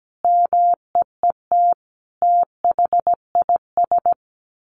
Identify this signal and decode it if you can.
Morse code, heard as a keyed tone, about 17 wpm, 700 Hz. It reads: MEET THIS
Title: Sentences from Top 1100 Words 17wpm